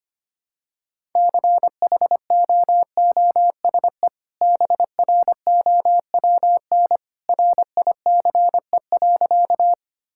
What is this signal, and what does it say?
Morse code, heard as a keyed tone, about 25 wpm, 700 Hz. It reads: CHOOSE BROWN RICE.
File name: Easy Sentences with Hints - 25wpm